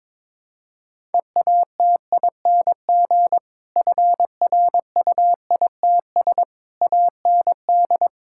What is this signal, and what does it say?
Morse code, heard as a keyed tone, about 22 wpm, 700 Hz. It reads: EATING FRUITS AND
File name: Difficult Sentences with Hints - 22wpm